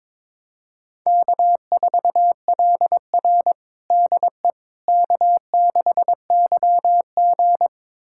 Morse code, heard as a keyed tone, about 22 wpm, 700 Hz. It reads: K4LR DE K6YG
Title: USA Callsign DE Callsign - 4 Characters - 22wpm